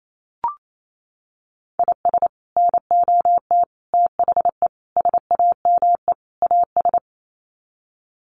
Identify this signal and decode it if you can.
Morse code, heard as a keyed tone, about 28 wpm, 700 Hz. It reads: IS NOT THE SAME AS
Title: Sets of 5 Words Encore 28wpm